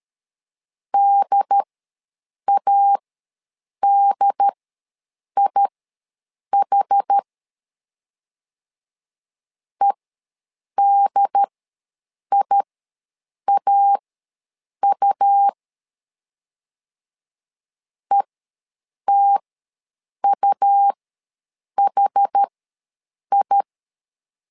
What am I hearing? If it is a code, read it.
Morse code, heard as a keyed tone, about 13 wpm, spaced as for 7 wpm, 800 Hz. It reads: DADIH EDIAU ETUHI
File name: LEZIONI / 2 UHDN / 2 UHD ws 12wpm 1722